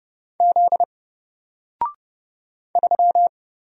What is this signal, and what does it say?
Morse code, heard as a keyed tone, about 30 wpm, 700 Hz. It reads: Z 3
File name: Single-Letter-Number-30wpm